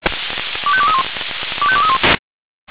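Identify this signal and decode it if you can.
Five-tone selective call (ZVEI), heard as quick runs of five tones, tones of about 70 ms. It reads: 25231 25231
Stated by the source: ZVEI five-tone call 25220 received by SDR.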